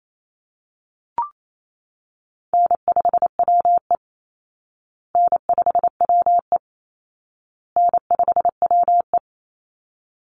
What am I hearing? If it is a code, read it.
Morse code, heard as a keyed tone, about 28 wpm, 700 Hz. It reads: N5WE N5WE N5WE
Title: Call Signs Repeated 2x - 28wpm